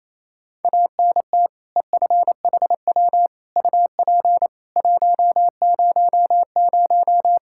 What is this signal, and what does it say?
Morse code, heard as a keyed tone, about 28 wpm, 700 Hz. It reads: ANT EFHW UP 100
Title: Practice QSO - Exchange 2 - Info - 28wpm